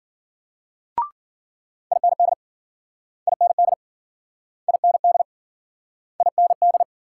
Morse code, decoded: IND IND IND IND

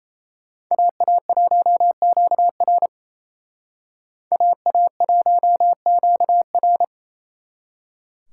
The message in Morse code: AA1QR AA1QR

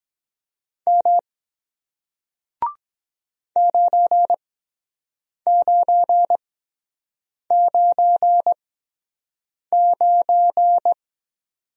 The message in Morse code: M 9999

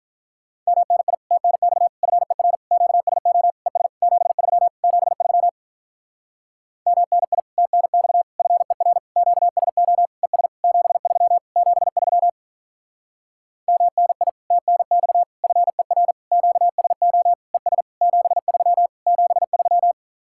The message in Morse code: MNI TNX FER QSO ES 73 73 MNI TNX FER QSO ES 73 73 MNI TNX FER QSO ES 73 73

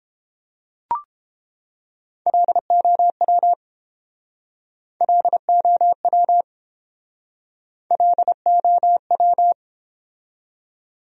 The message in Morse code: LOW LOW LOW

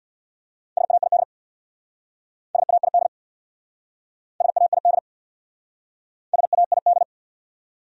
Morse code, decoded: SAID SAID SAID SAID

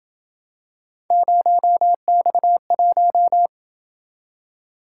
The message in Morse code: 0X1

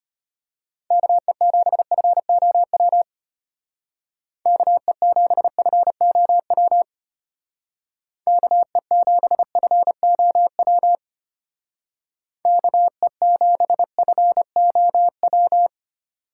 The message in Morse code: KE7FOW KE7FOW KE7FOW KE7FOW

